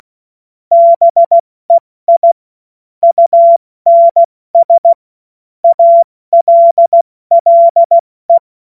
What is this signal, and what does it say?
Morse code, heard as a keyed tone, about 16 wpm, spaced as for 14 wpm, 680 Hz. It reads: BEI UNS ALLE
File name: Morsekurs / Witze / Witz 30. August 2022 70 F80